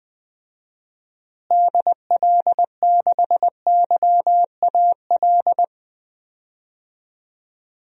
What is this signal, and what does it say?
Morse code, heard as a keyed tone, about 20 wpm, 700 Hz. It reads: DL6YAL
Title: DXCC-100 Call Signs - 6 Characters - Repeated 3x - 20wpm